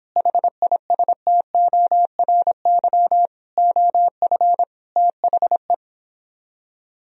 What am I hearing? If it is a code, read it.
Morse code, heard as a keyed tone, about 26 wpm, 700 Hz. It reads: HISTORY OF THE